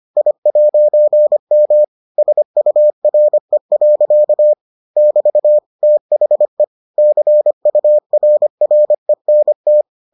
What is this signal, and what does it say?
Morse code, heard as a keyed tone, about 25 wpm, 600 Hz. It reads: I'M SURE. <BT> THE CURRENT